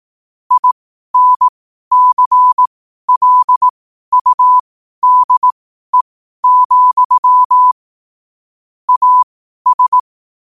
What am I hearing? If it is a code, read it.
Morse code, heard as a keyed tone, about 18 wpm, spaced as for 12 wpm, 1000 Hz. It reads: INCLUDE, AS
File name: qotdfiles / 12wpm